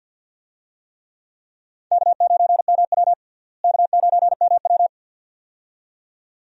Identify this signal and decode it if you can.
Morse code, heard as a keyed tone, about 52 wpm, 690 Hz. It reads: K9MW K9MW
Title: USA Call Signs - 4 Characters - Repeated 2x - 50wpm